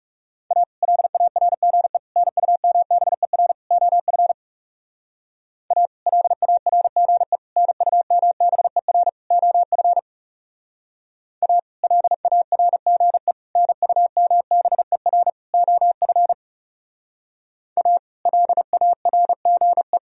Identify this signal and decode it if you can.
Morse code, heard as a keyed tone, about 38 wpm, 700 Hz. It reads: A LARGE NUMBER OF A LARGE NUMBER OF A LARGE NUMBER OF A LARGE